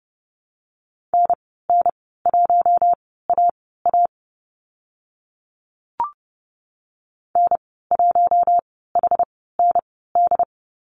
Morse code, heard as a keyed tone, about 31 wpm, spaced as for 17 wpm, 700 Hz. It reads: NN1AA N1HND
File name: Lesson 15g - Number 5 - Call Signs 30wpm